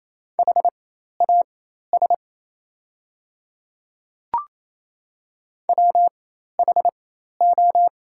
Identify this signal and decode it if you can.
Morse code, heard as a keyed tone, about 28 wpm, spaced as for 13 wpm, 700 Hz. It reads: HAS WHO
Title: 4x Character Spacing - Top 200 Words - 28wpm